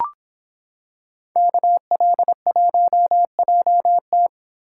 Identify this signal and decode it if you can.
Morse code, heard as a keyed tone, about 26 wpm, 700 Hz. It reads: KL1JT